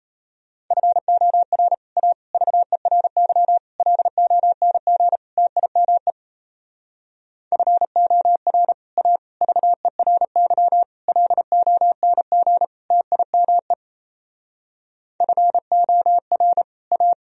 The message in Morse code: FOR A VERY LONG TIME FOR A VERY LONG TIME FOR A

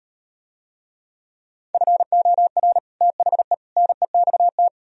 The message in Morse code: FOR THE NEXT

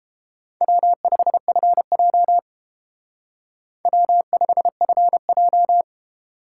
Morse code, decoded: W5FJ W5FJ

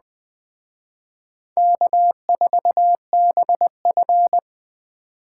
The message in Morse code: K4BF

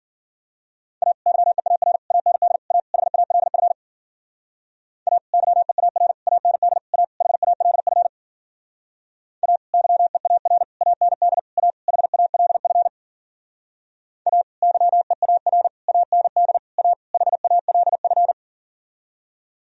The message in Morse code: A YEAR AND A HALF A YEAR AND A HALF A YEAR AND A HALF A YEAR AND A HALF

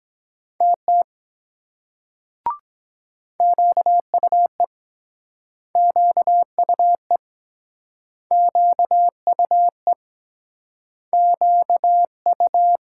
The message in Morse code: TT QUE QUE QUE QU

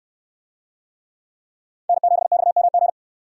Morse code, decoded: N7XMG

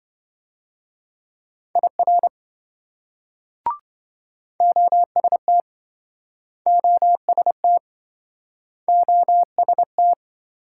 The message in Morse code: IR OST OST OST

